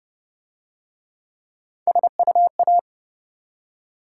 Morse code, heard as a keyed tone, about 33 wpm, 700 Hz. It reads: SUA